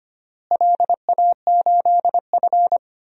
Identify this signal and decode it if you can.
Morse code, heard as a keyed tone, about 25 wpm, 700 Hz. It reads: LA8F
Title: DXCC-100 Call Signs - 4 Characters - Repeated 3x - 25wpm